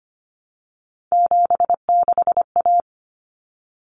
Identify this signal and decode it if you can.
Morse code, heard as a keyed tone, about 25 wpm, 700 Hz. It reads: 76A
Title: Common Cut Numbers - 3-Digit 25wpm